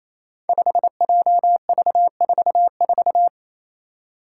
Morse code, heard as a keyed tone, about 28 wpm, 700 Hz. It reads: 5JV44
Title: Five Challenging Letter-Number Combinations 28wpm